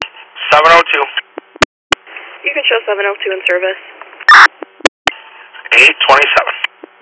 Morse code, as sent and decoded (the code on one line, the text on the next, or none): none